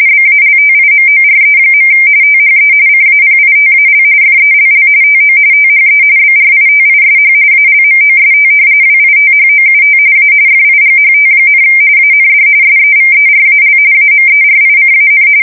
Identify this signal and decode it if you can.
RTTY, about 45.45 baud. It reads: BALTIMORE. THE CAPITAL FELL TO THE BRITISH ALMOST WITHOUT A FIGHT, AND SEVERAL AMERICAN
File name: fsk170 / usnavy2